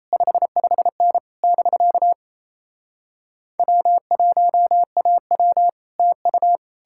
Morse code, heard as a keyed tone, about 38 wpm, 700 Hz. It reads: 55N <BK> W1AW TU